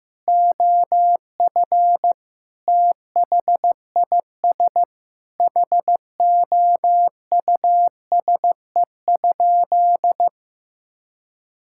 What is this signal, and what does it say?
Morse code, heard as a keyed tone, about 16 wpm, 700 Hz. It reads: OF THIS HOUSE?
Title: Sentences from Top 1600 Words 15wpm